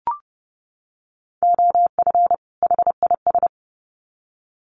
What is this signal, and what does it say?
Morse code, heard as a keyed tone, about 30 wpm, 700 Hz. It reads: OF HIS
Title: Sets of 2 Words 30wpm